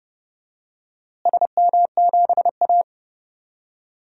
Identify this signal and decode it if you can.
Morse code, heard as a keyed tone, about 32 wpm, 700 Hz. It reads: SM7A
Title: DXCC-100 Call Signs - 4 Characters - Repeated 3x - 30wpm